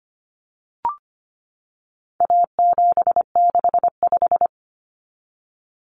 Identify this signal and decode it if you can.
Morse code, heard as a keyed tone, about 25 wpm, 700 Hz. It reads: A765
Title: Common Cut Numbers - 4-Digit 25wpm